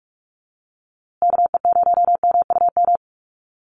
Morse code, heard as a keyed tone, about 44 wpm, 700 Hz. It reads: KE0MUM
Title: USA Call Signs Repeated 3x 45wpm